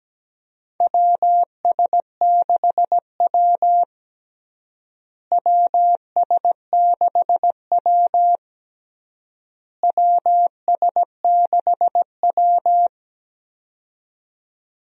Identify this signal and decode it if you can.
Morse code, heard as a keyed tone, about 17 wpm, 700 Hz. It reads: WS6W WS6W WS6W
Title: USA Call Signs - 4 Characters - Repeated 3x - 17wpm